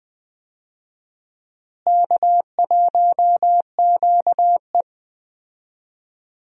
Morse code, heard as a keyed tone, about 21 wpm, 700 Hz. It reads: K1QE
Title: USA Call Signs - 4 Characters - Repeated 2x - 20wpm